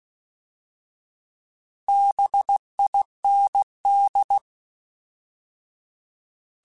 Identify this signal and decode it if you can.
Morse code, heard as a keyed tone, about 16 wpm, 800 Hz. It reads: BIND
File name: Alphabetical / Alphabetical mp3s / Bs @15